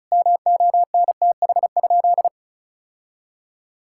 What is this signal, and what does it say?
Morse code, heard as a keyed tone, about 36 wpm, 700 Hz. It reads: MONTH?
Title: Sentences from Top 1000 Words - Review Part 5 - 35wpm